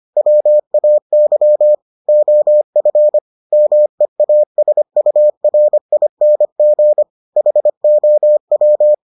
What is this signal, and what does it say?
Morse code, heard as a keyed tone, about 25 wpm, 600 Hz. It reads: WAY OF MEASURING HOW